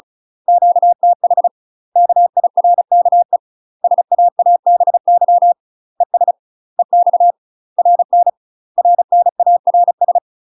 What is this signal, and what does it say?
Morse code, heard as a keyed tone, about 34 wpm, 700 Hz. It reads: QTH KIRKE SAABY ES EX RN RNARS
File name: UK MASTER QSOs TRAINING / 35WPM / UK MASTER QSOs 0000